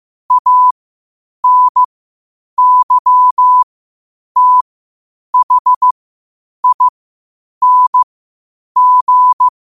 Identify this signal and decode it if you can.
Morse code, heard as a keyed tone, about 15 wpm, spaced as for 8 wpm, 1000 Hz. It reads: ANYTHING
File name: qotdfiles / 8wpm